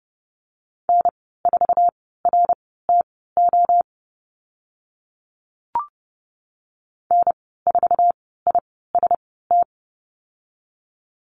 Morse code, heard as a keyed tone, about 30 wpm, spaced as for 17 wpm, 690 Hz. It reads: N4RTO N4IST
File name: Lesson 15g - Number 5 - Call Signs 30wpm